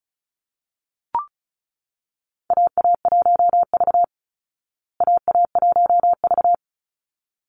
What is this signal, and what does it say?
Morse code, heard as a keyed tone, about 34 wpm, 700 Hz. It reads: AA1V AA1V
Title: Call Signs Repeated 3x - 35wpm